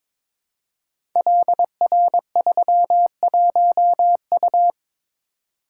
Morse code, heard as a keyed tone, about 23 wpm, 690 Hz. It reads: LR31U
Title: Code Groups - 5 Letter:Numbers - 22wpm